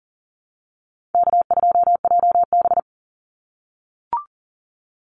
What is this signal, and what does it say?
Morse code, heard as a keyed tone, about 40 wpm, 700 Hz. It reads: K2JB
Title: NA SOTA Chaser Callsigns 40wpm